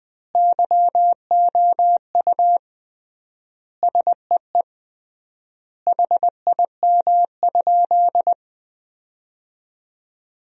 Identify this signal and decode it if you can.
Morse code, heard as a keyed tone, about 19 wpm, 700 Hz. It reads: YOU SEE HIM?
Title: 3x Word-Spacing - Sentences from Top 100 Words - 20wpm